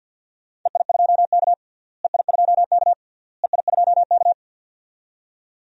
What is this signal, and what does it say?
Morse code, heard as a keyed tone, about 50 wpm, 700 Hz. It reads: EI2K EI2K EI2K
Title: DXCC-100 Call Signs - 4 Characters - Repeated 3x - 50wpm